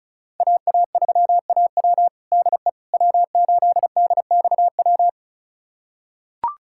Morse code, heard as a keyed tone, about 34 wpm, 700 Hz. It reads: AA3AW DE W8DXW
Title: USA Callsign DE Callsign - 5 Characters - 35wpm